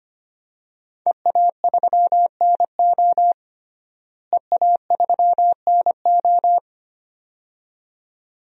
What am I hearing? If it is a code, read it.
Morse code, heard as a keyed tone, about 25 wpm, 700 Hz. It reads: EA3NO EA3NO
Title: DXCC-100 Call Signs - 5 Characters - Repeated 2x - 25wpm